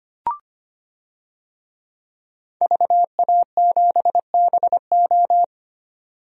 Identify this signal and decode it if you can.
Morse code, heard as a keyed tone, about 25 wpm, 700 Hz. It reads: VA7BO
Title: DXCC-100 Call Signs - 5 Characters - 25wpm